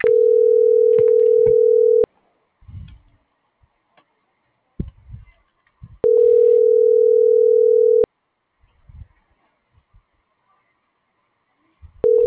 Ringback tone (North America), 440 + 480 Hz.